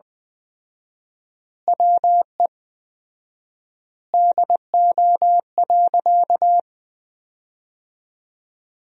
Morse code, heard as a keyed tone, about 20 wpm, 700 Hz. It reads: WE DO.
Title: 4x Word-Spacing - Sentences from Top 100 Words - 20wpm